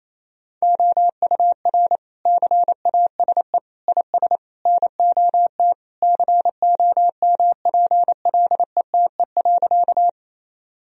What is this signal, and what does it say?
Morse code, heard as a keyed tone, about 28 wpm, 700 Hz. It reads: OUR CASE IS NOT COMPLETE.
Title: Sentences from Top 1000 Words - Review Part 2 - 28wpm